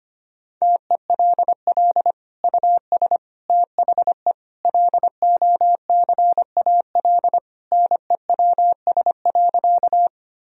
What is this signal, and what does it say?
Morse code, heard as a keyed tone, about 25 wpm, 700 Hz. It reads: TELL US THE LOCAL NEWS.
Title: Sentences from Top 1000 Words - Review Part 3 - 25wpm